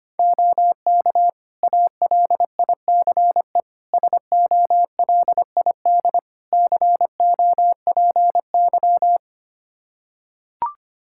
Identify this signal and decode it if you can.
Morse code, heard as a keyed tone, about 25 wpm, 700 Hz. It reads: OK ALICE SOLID COPY
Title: Practice QSO - Exchange 3 - More Info - 25wpm